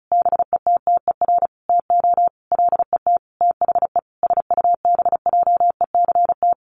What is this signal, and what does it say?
Morse code, heard as a keyed tone, about 34 wpm, 700 Hz. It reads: BETTER TO LET THE SUBJECT
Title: Sentences from Top 1400 Words 35wpm